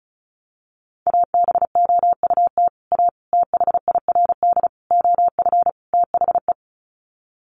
Morse code, heard as a keyed tone, about 34 wpm, 700 Hz. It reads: ABOUT A THIRD OF THE
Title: Sets of 5 Words Encore2 35wpm